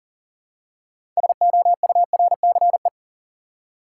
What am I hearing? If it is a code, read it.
Morse code, heard as a keyed tone, about 40 wpm, 700 Hz. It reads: SOURCE
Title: Six Letter Words - 40wpm